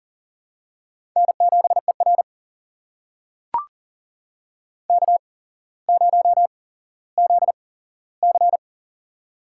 Morse code, heard as a keyed tone, about 37 wpm, spaced as for 17 wpm, 700 Hz. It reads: N7ER K0ZC
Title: USA Call Signs - 4 Characters - 8x Character Spacing - 40wpm